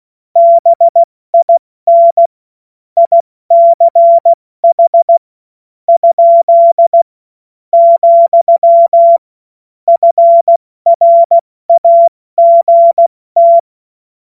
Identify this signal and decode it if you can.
Morse code, heard as a keyed tone, about 16 wpm, spaced as for 14 wpm, 680 Hz. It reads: BIN ICH ? , FRAGT
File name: Morsekurs / Witze / Witz 13. September 2022 70 F80